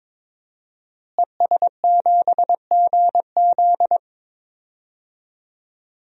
Morse code, heard as a keyed tone, about 22 wpm, 700 Hz. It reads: ES7GZ
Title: DXCC-100 Call Signs - 5 Characters - Repeated 2x - 22wpm